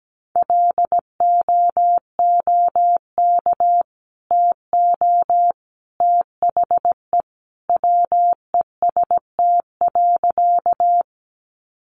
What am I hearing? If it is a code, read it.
Morse code, heard as a keyed tone, about 17 wpm, 700 Hz. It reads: LOOK TO THE WEST.
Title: Sentences from Top 400 Words 17wpm